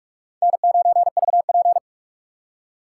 Morse code, heard as a keyed tone, about 44 wpm, 700 Hz. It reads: N9VP
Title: USA Call Signs - 4 Characters - 4x Character Spacing - 45wpm